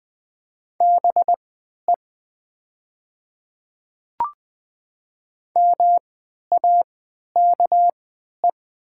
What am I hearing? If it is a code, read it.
Morse code, heard as a keyed tone, about 20 wpm, spaced as for 11 wpm, 700 Hz. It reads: BE MAKE
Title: 3x Character Spacing - Top 300 Words - 20wpm